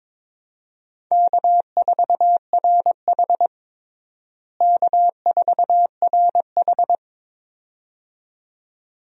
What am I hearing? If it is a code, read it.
Morse code, heard as a keyed tone, about 22 wpm, 700 Hz. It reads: K4RH K4RH
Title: USA Call Signs - 4 Characters - Repeated 2x - 22wpm